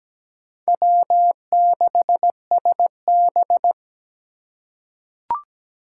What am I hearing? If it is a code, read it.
Morse code, heard as a keyed tone, about 17 wpm, 700 Hz. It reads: W6SB
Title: USA Call Signs - 4 Characters - 3x Character Spacing - 17wpm